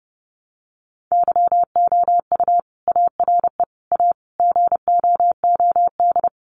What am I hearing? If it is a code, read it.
Morse code, heard as a keyed tone, about 30 wpm, 700 Hz. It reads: YOU ARE A GOOD
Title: Sentences from Top 200 Words 30wpm